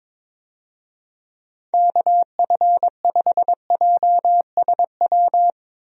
Morse code, heard as a keyed tone, about 22 wpm, 700 Hz. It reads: KF5JSW
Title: USA Call Signs - 6 Characters - Repeated 3x - 22wpm